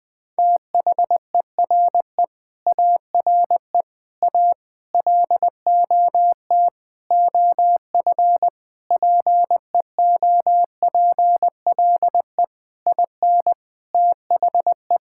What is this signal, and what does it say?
Morse code, heard as a keyed tone, about 20 wpm, 700 Hz. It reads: THERE ARE A LOT OF PEOPLE IN THE
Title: Sentences with 5-Word Phrases - Spoken Beginning - 20wpm